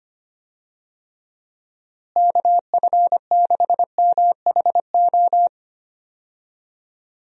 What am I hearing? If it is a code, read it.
Morse code, heard as a keyed tone, about 25 wpm, 700 Hz. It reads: KF6MHO
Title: USA Call Signs - 6 Characters - Repeated 2x - 25wpm